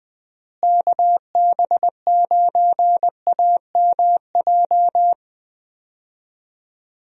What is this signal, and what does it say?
Morse code, heard as a keyed tone, about 21 wpm, 700 Hz. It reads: KB9AMJ
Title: USA Call Signs - 6 Characters - Repeated 3x - 20wpm